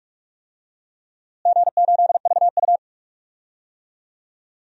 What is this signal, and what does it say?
Morse code, heard as a keyed tone, about 46 wpm, 700 Hz. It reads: G8VU
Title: DXCC-100 Call Signs - 4 Characters - Repeated 2x - 45wpm